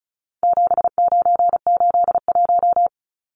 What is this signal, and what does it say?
Morse code, heard as a keyed tone, about 34 wpm, 700 Hz. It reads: 7981
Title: Numbers - 4-Digit 35wpm